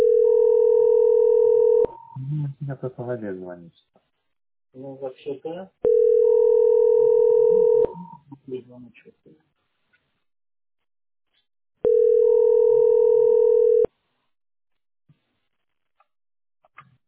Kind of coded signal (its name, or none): telephone call progress tone